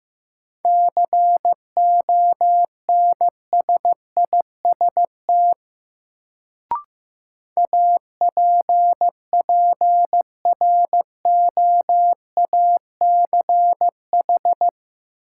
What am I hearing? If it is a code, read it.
Morse code, heard as a keyed tone, about 15 wpm, 700 Hz. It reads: CONSIST APPROACH
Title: Sentences from Top 1400 Words - New Word Drill Alt - 15wpm